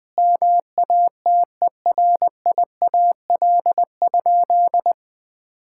Morse code, heard as a keyed tone, about 20 wpm, 700 Hz. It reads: MATERIAL?
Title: Sentences from Top 1000 Words - Review Part 5 - 20wpm